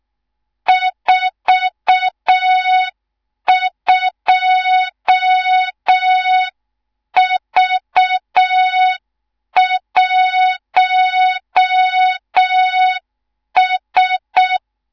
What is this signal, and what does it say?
Morse code, heard as a keyed tone, about 6 wpm, 3700 Hz. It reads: 42V1S
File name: Morze-HA2MN / hang / 18 IRASJELEK2